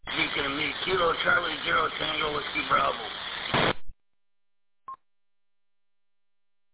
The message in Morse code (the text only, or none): none